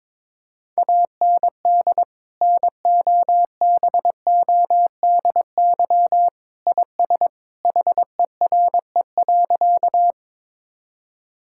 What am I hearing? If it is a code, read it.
Morse code, heard as a keyed tone, about 21 wpm, 700 Hz. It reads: AND NOBODY IS HERE.
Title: Sentences from Top 1600 Words 22wpm